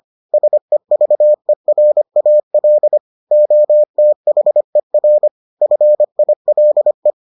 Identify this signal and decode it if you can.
Morse code, heard as a keyed tone, about 25 wpm, 600 Hz. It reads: SEVERAL OTHER FILE